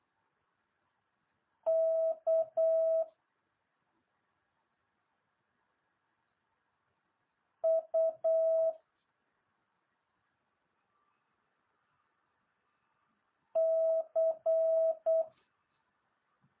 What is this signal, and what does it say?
Morse code, heard as a keyed tone, about 8 wpm, 650 Hz. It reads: K U C